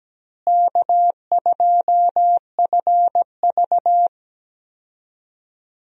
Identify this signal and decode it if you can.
Morse code, heard as a keyed tone, about 17 wpm, 700 Hz. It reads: K2FV
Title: USA Call Signs - 4 Characters - Repeated 3x - 17wpm